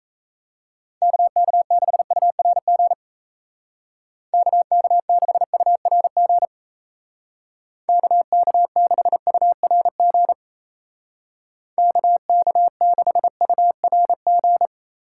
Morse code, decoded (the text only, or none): KK6URG KK6URG KK6URG KK6URG